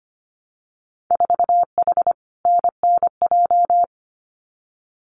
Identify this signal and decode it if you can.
Morse code, heard as a keyed tone, about 25 wpm, 700 Hz. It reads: 4H NNJ
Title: Winter Field Day Exchanges Encore 25wpm